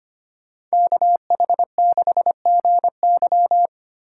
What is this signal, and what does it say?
Morse code, heard as a keyed tone, about 33 wpm, 700 Hz. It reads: KH6GY